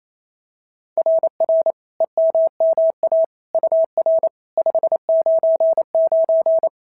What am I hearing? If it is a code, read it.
Morse code, heard as a keyed tone, about 28 wpm, 650 Hz. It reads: RR EMMA UR 599
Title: SOTA QSO Practice 28wpm